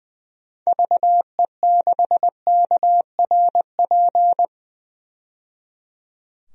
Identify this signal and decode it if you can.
Morse code, heard as a keyed tone, about 20 wpm, 700 Hz. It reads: VE6KRP
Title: DXCC-100 Call Signs - 6 Characters - 20wpm